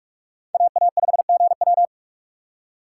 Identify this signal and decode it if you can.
Morse code, heard as a keyed tone, about 44 wpm, 700 Hz. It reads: AA5GW